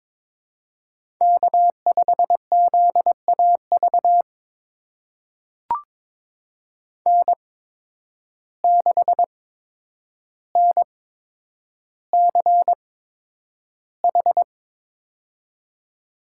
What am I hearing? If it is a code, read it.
Morse code, heard as a keyed tone, about 22 wpm, spaced as for 9 wpm, 700 Hz. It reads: K5ZAV N6NCH
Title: USA Call Signs - 5 Characters - 8x Character Spacing - 22wpm